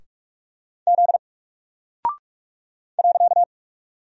Morse code, heard as a keyed tone, about 46 wpm, 700 Hz. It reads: Z .
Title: Single Character (Letter:Number:Punctuation:Prosign) 45wpm